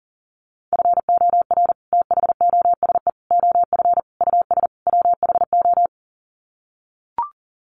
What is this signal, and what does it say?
Morse code, heard as a keyed tone, about 40 wpm, 700 Hz. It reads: FOR THOSE OF US WHO
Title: Sets of 5 Words Encore 40wpm